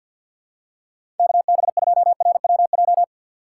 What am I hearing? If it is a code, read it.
Morse code, heard as a keyed tone, about 53 wpm, 700 Hz. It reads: KB2RWJ
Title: USA Call Signs - 6 Characters - 4x Character Spacing - 50wpm